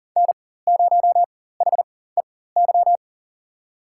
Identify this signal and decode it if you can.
Morse code, heard as a keyed tone, about 40 wpm, spaced as for 19 wpm, 700 Hz. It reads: N0HEY
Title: USA Call Signs - 5 Characters - 4x Character Spacing - 40wpm